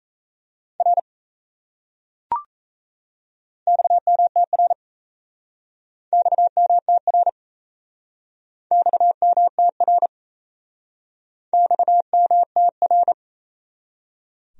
Morse code, heard as a keyed tone, about 37 wpm, 700 Hz. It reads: R XMTR XMTR XMTR XMTR